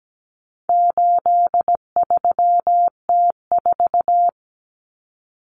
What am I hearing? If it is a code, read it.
Morse code, heard as a keyed tone, about 17 wpm, 700 Hz. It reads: 83T4
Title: Common Cut Numbers - 4-Digit 17wpm